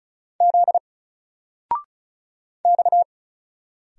Morse code, heard as a keyed tone, about 36 wpm, 700 Hz. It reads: Z X
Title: Single Character (Letter:Number:Punctuation:Prosign) 35wpm